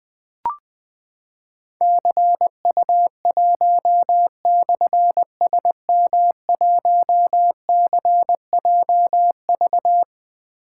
Morse code, heard as a keyed tone, about 21 wpm, 700 Hz. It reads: CU1/SM1CJV
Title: Difficult Call Signs 20wpm